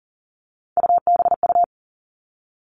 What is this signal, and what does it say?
Morse code, heard as a keyed tone, about 40 wpm, 700 Hz. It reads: UBU
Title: All Cut Numbers - 3-Digit 40wpm